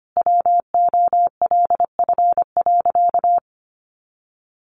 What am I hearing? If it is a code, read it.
Morse code, heard as a keyed tone, about 25 wpm, 700 Hz. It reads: WOLF.
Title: Additional Easy Phrases 25wpm